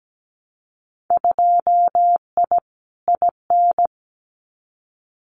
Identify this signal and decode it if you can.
Morse code, heard as a keyed tone, about 17 wpm, 700 Hz. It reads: 2I IN
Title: Winter Field Day Exchanges Encore 17wpm